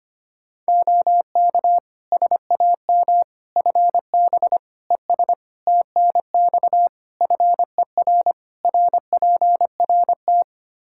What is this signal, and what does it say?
Morse code, heard as a keyed tone, about 25 wpm, 700 Hz. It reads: OK SAM FB ES TNX FER RPRT
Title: Practice QSO - Exchange 2 - Info - 25wpm